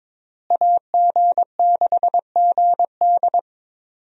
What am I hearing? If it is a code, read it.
Morse code, heard as a keyed tone, about 22 wpm, 700 Hz. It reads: AG6GD
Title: USA Call Signs - 5 Characters - Repeated 2x - 22wpm